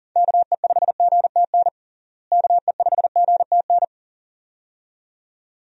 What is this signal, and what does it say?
Morse code, heard as a keyed tone, about 40 wpm, 700 Hz. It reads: KE5GTN KE5GTN
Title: USA Call Signs - 6 Characters - Repeated 2x - 40wpm